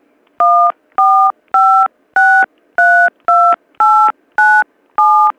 DTMF, heard as telephone touch tones, tones of about 330 ms. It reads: 145632897